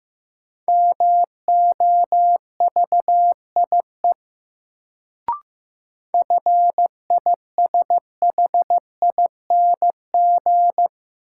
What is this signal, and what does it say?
Morse code, heard as a keyed tone, about 15 wpm, 700 Hz. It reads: MOVIE FISHING
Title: Sentences from Top 1000 Words - Review Part 5 - New Word Drill - 15wpm